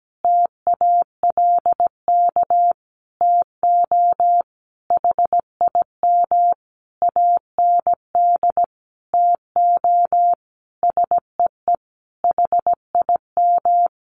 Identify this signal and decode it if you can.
Morse code, heard as a keyed tone, about 17 wpm, 700 Hz. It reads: TALK TO HIM AND TO SEE HIM
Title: Sentences from Top 1400 Words 17wpm